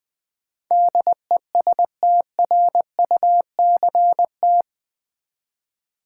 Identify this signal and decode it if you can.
Morse code, heard as a keyed tone, about 20 wpm, 700 Hz. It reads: DESTRUCT
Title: Build A Word - 4 Letters At A Time - 11-Letter Words - 20wpm